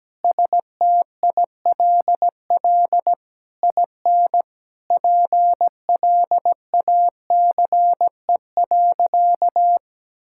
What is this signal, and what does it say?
Morse code, heard as a keyed tone, about 17 wpm, 700 Hz. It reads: STILL IN PLACE.
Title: Sentences from Top 1000 Words - Review Part 6 - 17wpm